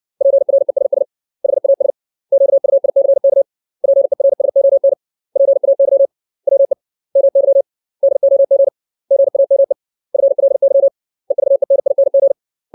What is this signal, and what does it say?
Morse code, heard as a keyed tone, about 60 wpm, 550 Hz. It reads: PRESS HAS QUICK PERSON PAY WE MY DOG GAME FLY EVENING
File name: 632 words / 632 Words 60WPM